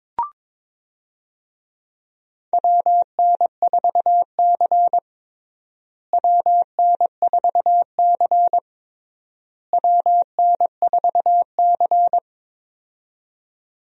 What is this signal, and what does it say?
Morse code, heard as a keyed tone, about 22 wpm, 700 Hz. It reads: WN4C WN4C WN4C
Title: USA Call Signs - 4 Characters - Repeated 3x - 22wpm